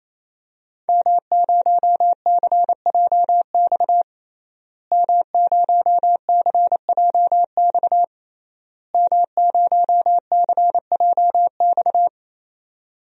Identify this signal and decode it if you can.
Morse code, heard as a keyed tone, about 28 wpm, 690 Hz. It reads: M0CJX M0CJX M0CJX
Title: DXCC-100 Call Signs - 5 Characters - Repeated 3x - 28wpm